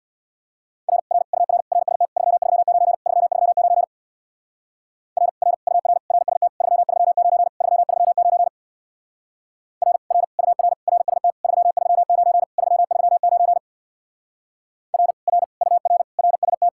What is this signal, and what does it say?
Morse code, heard as a keyed tone, about 66 wpm, 700 Hz. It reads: R R UR RST 339 339 R R UR RST 339 339 R R UR RST 339 339 R R UR RST